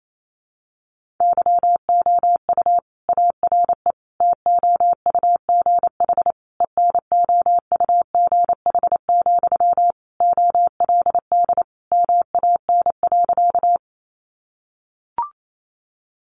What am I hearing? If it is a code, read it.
Morse code, heard as a keyed tone, about 28 wpm, 700 Hz. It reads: YOU ARE TOUGH ENOUGH, OLD MAN.
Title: Sentences from Top 1200 Words 28wpm